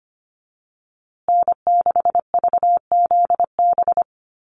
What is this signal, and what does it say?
Morse code, heard as a keyed tone, about 25 wpm, 700 Hz. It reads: N6VZB
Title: USA Call Signs Repeated 3x 25wpm